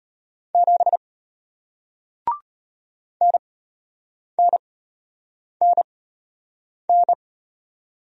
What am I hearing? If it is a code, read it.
Morse code, heard as a keyed tone, about 33 wpm, spaced as for 8 wpm, 700 Hz. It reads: 7 NNNN